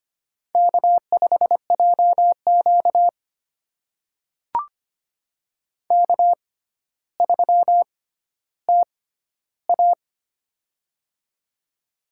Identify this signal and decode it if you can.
Morse code, heard as a keyed tone, about 25 wpm, spaced as for 13 wpm, 700 Hz. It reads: K5JQ K3TA
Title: USA Call Signs - 4 Characters - 6x Character Spacing - 25wpm